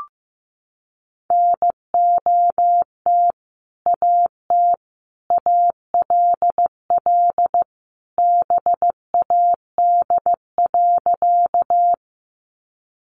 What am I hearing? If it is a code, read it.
Morse code, heard as a keyed tone, about 15 wpm, 700 Hz. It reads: NOT AT ALL BAD.
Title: Sentences from Top 700 Words 15wpm